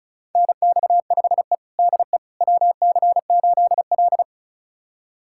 Morse code, heard as a keyed tone, about 36 wpm, 700 Hz. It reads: NX5E DE WC8L
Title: USA Callsign DE Callsign - 4 Characters - 35wpm